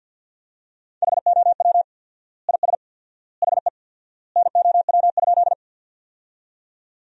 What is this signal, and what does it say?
Morse code, heard as a keyed tone, about 50 wpm, 700 Hz. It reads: HOW IS HE NOW?
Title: 4x Word-Spacing - Sentences from Top 100 Words - 50wpm